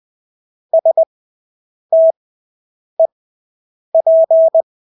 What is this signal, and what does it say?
Morse code, heard as a keyed tone, about 20 wpm, spaced as for 8 wpm, 650 Hz. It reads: STEP
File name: Engelske ord 20/8 wpm